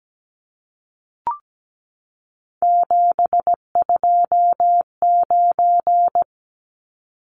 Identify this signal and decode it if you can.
Morse code, heard as a keyed tone, about 17 wpm, 700 Hz. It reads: 729
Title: Numbers - 3-Digit 17wpm